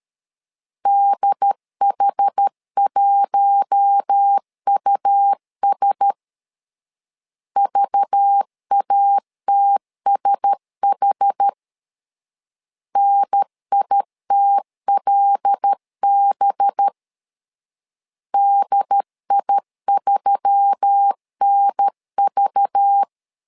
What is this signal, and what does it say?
Morse code, heard as a keyed tone, about 13 wpm, 800 Hz. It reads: DH1US VATSH NITLB DI3NV
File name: LEZIONI / 5 LBVC / 5 LBV 12wpm 1855